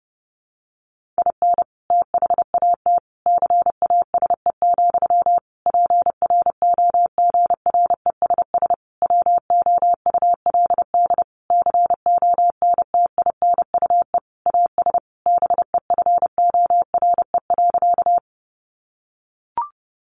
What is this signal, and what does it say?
Morse code, heard as a keyed tone, about 30 wpm, 700 Hz. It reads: IN THAT CASE, PROGRESS WOULD CONTINUE AS BEFORE.
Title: Sentences from Top 1400 Words 30wpm